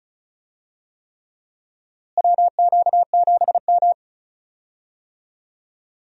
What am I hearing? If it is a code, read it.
Morse code, heard as a keyed tone, about 34 wpm, 700 Hz. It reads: WQ7M
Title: USA Call Signs - 4 Characters - Repeated 3x - 35wpm